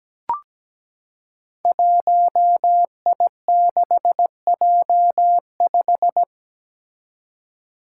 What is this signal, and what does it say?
Morse code, heard as a keyed tone, about 17 wpm, 700 Hz. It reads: 1I6J5
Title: Five Challenging Letter-Number Combinations 17wpm